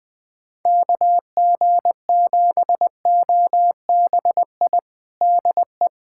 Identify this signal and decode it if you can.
Morse code, heard as a keyed tone, about 20 wpm, 700 Hz. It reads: KG7OBI DE
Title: USA Callsign DE Callsign - 6 Characters - 20wpm